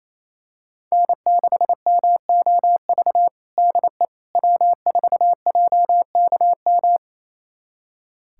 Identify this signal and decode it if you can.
Morse code, heard as a keyed tone, about 27 wpm, 700 Hz. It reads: N6MOV DE W4JKM
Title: USA Callsign DE Callsign - 5 Characters - 28wpm